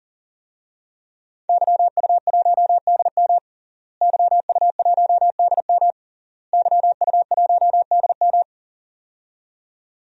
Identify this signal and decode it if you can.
Morse code, heard as a keyed tone, about 40 wpm, 700 Hz. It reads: YU1DM YU1DM YU1DM
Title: DXCC-100 Call Signs - 5 Characters - Repeated 3x - 40wpm